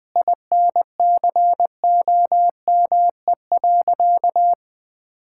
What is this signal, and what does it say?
Morse code, heard as a keyed tone, about 20 wpm, 700 Hz. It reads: INCOME.
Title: Sentences from Top 1000 Words - Review Part 6 - 20wpm